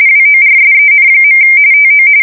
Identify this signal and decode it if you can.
RTTY, about 45.45 baud. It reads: WORSEN. TO
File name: fsk170 / battleshipnj